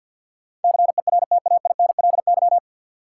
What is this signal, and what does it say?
Morse code, heard as a keyed tone, about 53 wpm, 700 Hz. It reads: CERTAINLY
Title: Sentences from Top 900 Words - New Word Drill - 50wpm